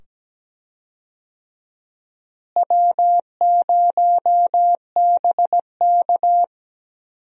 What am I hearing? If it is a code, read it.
Morse code, heard as a keyed tone, about 17 wpm, 700 Hz. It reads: W0BK
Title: USA Call Signs - 4 Characters - Repeated 2x - 17wpm